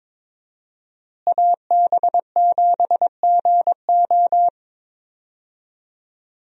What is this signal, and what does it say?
Morse code, heard as a keyed tone, about 22 wpm, 700 Hz. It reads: AB7GO
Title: USA Call Signs - 5 Characters - Repeated 2x - 22wpm